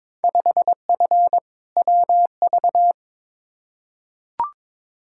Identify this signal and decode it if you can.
Morse code, heard as a keyed tone, about 22 wpm, 700 Hz. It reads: 5F WV
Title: Field Day Exchanges Encore - 22wpm